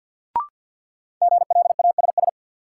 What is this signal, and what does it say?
Morse code, heard as a keyed tone, about 51 wpm, 700 Hz. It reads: GLASS
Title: Sentences from Top 1000 Words - Review Part 3 - New Word Drill - 50wpm